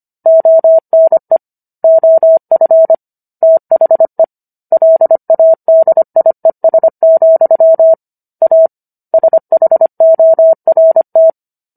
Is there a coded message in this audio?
Yes: Morse code from 0.3 to 11.3 s, heard as a keyed tone, about 25 wpm, 650 Hz. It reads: ONE OF THE LADIES, A SHORT